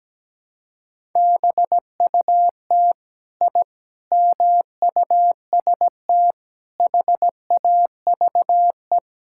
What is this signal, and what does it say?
Morse code, heard as a keyed tone, about 17 wpm, 700 Hz. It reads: BUT I MUST HAVE
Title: Sentences from Top 1000 Words - Review Part 5 - 17wpm